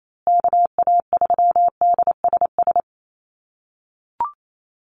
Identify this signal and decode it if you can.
Morse code, heard as a keyed tone, about 28 wpm, 700 Hz. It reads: KA3DSS
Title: USA Call Signs Repeated 3x 28wpm